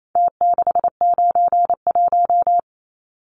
Morse code, heard as a keyed tone, about 28 wpm, 700 Hz. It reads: T691
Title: Numbers - Contest Style - 4-Digit 28wpm